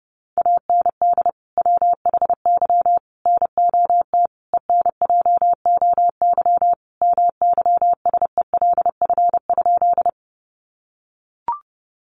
Morse code, heard as a keyed tone, about 30 wpm, 700 Hz. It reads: AND WHY NOT ENJOY MYSELF?
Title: Sentences from Top 1300 Words 30wpm